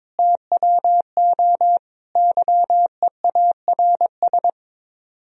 Morse code, heard as a keyed tone, about 23 wpm, 700 Hz. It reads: TWO YEARS
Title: Speed-Racing - Sets of 5 Words 22wpm